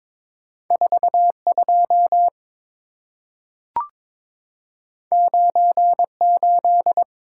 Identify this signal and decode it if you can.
Morse code, heard as a keyed tone, about 22 wpm, 690 Hz. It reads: 42 98
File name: practice / 22wpm / Two Letter-Number Combinations - 22wpm